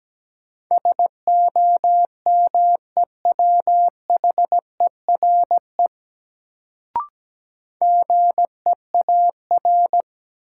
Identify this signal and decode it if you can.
Morse code, heard as a keyed tone, about 17 wpm, 700 Hz. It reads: SOMEWHERE GEAR
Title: Sentences from Top 1400 Words - New Word Drill - 17wpm